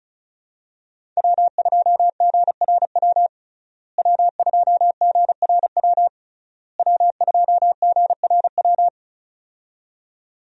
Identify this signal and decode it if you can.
Morse code, heard as a keyed tone, about 34 wpm, 700 Hz. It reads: W2GRW W2GRW W2GRW
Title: USA Call Signs - 5 Characters - Repeated 3x - 35wpm